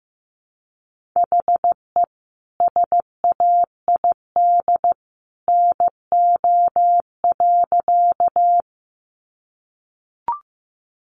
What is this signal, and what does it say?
Morse code, heard as a keyed tone, about 15 wpm, 700 Hz. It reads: HE SAID NO.
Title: Sentences from Top 100 Words 15wpm